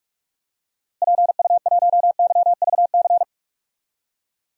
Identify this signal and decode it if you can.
Morse code, heard as a keyed tone, about 46 wpm, 700 Hz. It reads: PU1YVC
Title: DXCC-100 Call Signs - 6 Characters - Repeated 3x - 45wpm